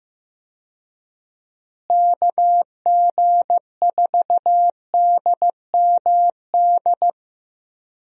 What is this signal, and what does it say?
Morse code, heard as a keyed tone, about 15 wpm, 700 Hz. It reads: KG4DMD
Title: USA Call Signs - 6 Characters - Repeated 3x - 15wpm